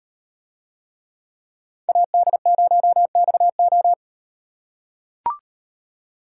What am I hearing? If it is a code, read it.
Morse code, heard as a keyed tone, about 36 wpm, 700 Hz. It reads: AD0XO